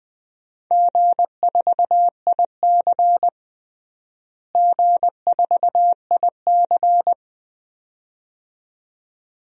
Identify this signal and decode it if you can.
Morse code, heard as a keyed tone, about 20 wpm, 700 Hz. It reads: G4IC G4IC
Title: DXCC-100 Call Signs - 4 Characters - Repeated 2x - 20wpm